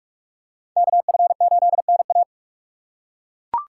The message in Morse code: KF8NA

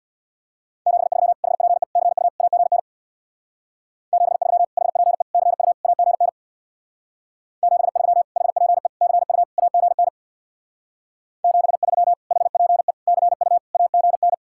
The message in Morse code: 73 HPE CU AGN 73 HPE CU AGN 73 HPE CU AGN 73 HPE CU AGN

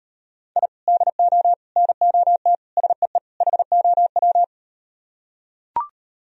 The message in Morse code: I DO NOT SEE HOW

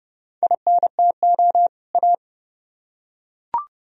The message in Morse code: INTO A

